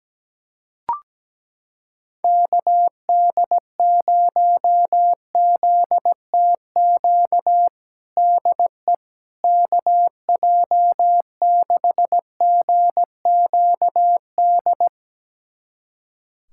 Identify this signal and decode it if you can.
Morse code, heard as a keyed tone, about 17 wpm, 700 Hz. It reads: KD0ZTQ DE KJ6GQD
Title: USA Callsign DE Callsign 17wpm